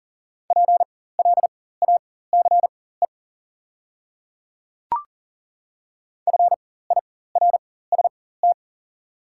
Morse code, heard as a keyed tone, about 39 wpm, spaced as for 19 wpm, 690 Hz. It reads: PLACE FIRST
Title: 4x Character Spacing - Top 200 Words - 40wpm